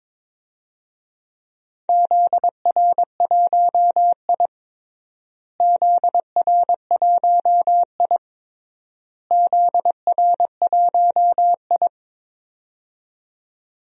Morse code, heard as a keyed tone, about 22 wpm, 700 Hz. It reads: ZR1I ZR1I ZR1I
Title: DXCC-100 Call Signs - 4 Characters - Repeated 3x - 22wpm